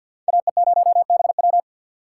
Morse code, heard as a keyed tone, about 48 wpm, 700 Hz. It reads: AE0BW
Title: USA Call Signs - 3x Character Spacing - 50wpm